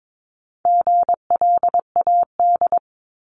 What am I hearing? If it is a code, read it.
Morse code, heard as a keyed tone, about 22 wpm, 700 Hz. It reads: GLAD
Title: Multi-letter Phonemes 22wpm